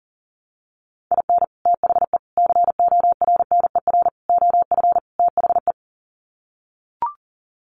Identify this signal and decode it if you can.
Morse code, heard as a keyed tone, about 40 wpm, 700 Hz. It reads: IN THE CORNER OF THE
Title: Sets of 5 Words Encore 40wpm